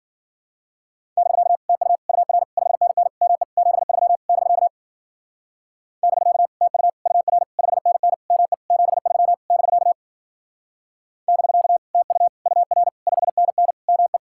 Morse code, heard as a keyed tone, about 50 wpm, 700 Hz. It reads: <BK> TU UR 5NN ME 73 <BK> <BK> TU UR 5NN ME 73 <BK> <BK> TU UR 5NN ME